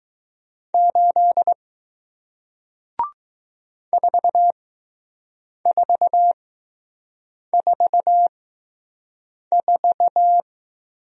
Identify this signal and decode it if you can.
Morse code, heard as a keyed tone, about 20 wpm, spaced as for 6 wpm, 700 Hz. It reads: 8 4444